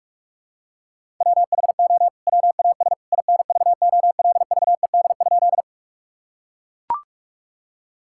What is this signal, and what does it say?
Morse code, heard as a keyed tone, about 41 wpm, 700 Hz. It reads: WHO WAS INVOLVED?
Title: Sentences from Top 1000 Words - Review Part 5 - 45wpm